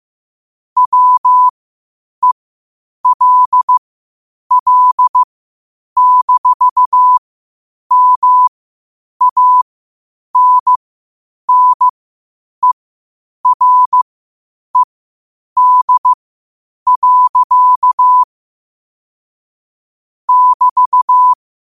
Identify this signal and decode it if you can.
Morse code, heard as a keyed tone, about 15 wpm, spaced as for 8 wpm, 1000 Hz. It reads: WELL-MANNERED. <BT>
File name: qotdfiles / 8wpm